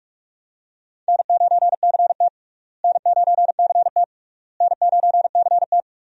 Morse code, heard as a keyed tone, about 42 wpm, 700 Hz. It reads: N9CT N9CT N9CT
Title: USA Call Signs - 4 Characters - Repeated 3x - 45wpm